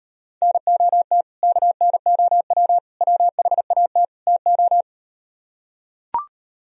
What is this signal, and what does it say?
Morse code, heard as a keyed tone, about 39 wpm, 700 Hz. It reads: NOT KNOW WHAT TO